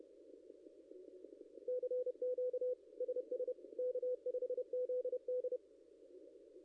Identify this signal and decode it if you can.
Morse code, heard as a keyed tone, about 31 wpm, 500 Hz. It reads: CQ SS K5ZD
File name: contests / ss-cw-k5tr-op-n5rz -0600